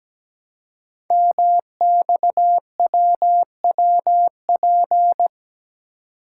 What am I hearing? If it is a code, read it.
Morse code, heard as a keyed tone, about 17 wpm, 700 Hz. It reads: MXWWP
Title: Code Groups - 5 Letters - 17wpm